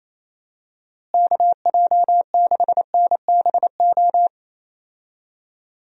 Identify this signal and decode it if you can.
Morse code, heard as a keyed tone, about 28 wpm, 700 Hz. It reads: KJ6NBO
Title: USA Call Signs - 6 Characters - Repeated 2x - 28wpm